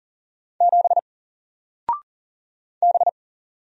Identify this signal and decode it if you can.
Morse code, heard as a keyed tone, about 40 wpm, 700 Hz. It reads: 7 B
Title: Single Character (Letter:Number:Punctuation:Prosign) 40wpm